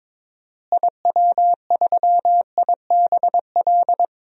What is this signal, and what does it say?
Morse code, heard as a keyed tone, about 22 wpm, 700 Hz. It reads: IW3IBL
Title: DXCC-100 Call Signs - 6 Characters - Repeated 3x - 22wpm